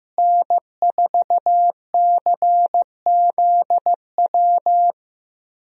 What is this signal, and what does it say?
Morse code, heard as a keyed tone, about 15 wpm, 700 Hz. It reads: N4CZW
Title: USA Call Signs - 5 Characters - Repeated 2x - 15wpm